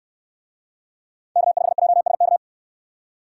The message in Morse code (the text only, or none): K5JSK